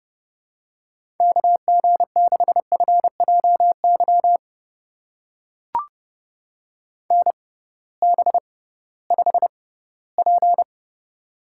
Morse code, heard as a keyed tone, about 30 wpm, spaced as for 18 wpm, 700 Hz. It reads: KG6FJY NB5P
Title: USA Call Signs - 6x Character Spacing - 30wpm